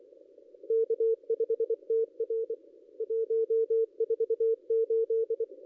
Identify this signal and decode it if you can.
Morse code, heard as a keyed tone, about 24 wpm, 440 Hz. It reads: K5TR 148
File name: contests / ss-cw-k5tr-op-n5rz -0400